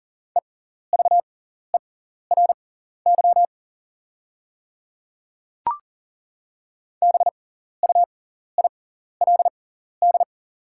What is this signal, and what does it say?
Morse code, heard as a keyed tone, about 42 wpm, spaced as for 14 wpm, 690 Hz. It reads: EVERY BUILD
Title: 6x Character Spacing - Top 500 Words - 40wpm